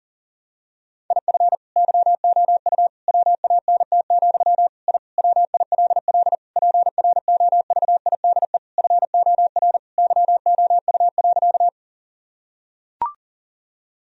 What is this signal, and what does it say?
Morse code, heard as a keyed tone, about 40 wpm, 700 Hz. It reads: IF YOU WANT, I WILL PROVIDE FOR YOU.
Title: Sentences from Top 1000 Words - Review Part 3 - 40wpm